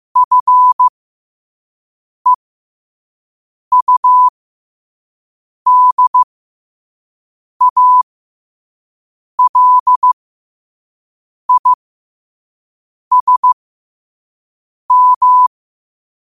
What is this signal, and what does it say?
Morse code, heard as a keyed tone, about 15 wpm, spaced as for 5 wpm, 1000 Hz. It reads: FEUDALISM
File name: qotdfiles / 5wpm